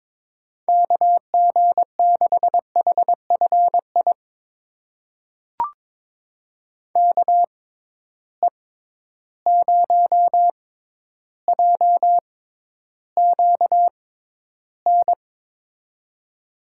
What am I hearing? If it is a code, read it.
Morse code, heard as a keyed tone, about 22 wpm, spaced as for 11 wpm, 700 Hz. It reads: KG6HFI KE0JQN
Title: USA Call Signs - 6 Characters - 6x Character Spacing - 22wpm